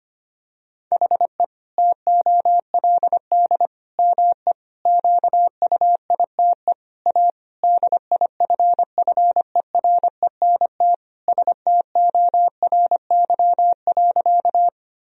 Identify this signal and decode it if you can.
Morse code, heard as a keyed tone, about 25 wpm, 700 Hz. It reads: HE TOLD ME QUITE A DIFFERENT STORY.
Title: Sentences from Top 1000 Words - Review Part 3 - 25wpm